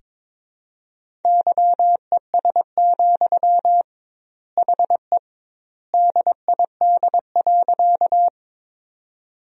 Morse code, heard as a keyed tone, about 22 wpm, 700 Hz. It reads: YES, HE DID.
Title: 2x Word-Spacing - Sentences from Top 500 Words - 22wpm